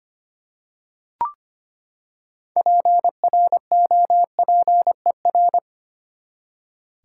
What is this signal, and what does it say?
Morse code, heard as a keyed tone, about 25 wpm, 700 Hz. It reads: PROPER
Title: Six Letter Words - 25wpm